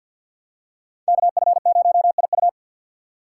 Morse code, heard as a keyed tone, about 47 wpm, 700 Hz. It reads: KF0IU
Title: USA Call Signs - 5 Characters - 3x Character Spacing - 50wpm